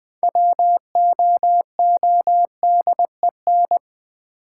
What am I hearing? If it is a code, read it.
Morse code, heard as a keyed tone, about 20 wpm, 700 Hz. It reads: WOODEN
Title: Six Letter Words - 20wpm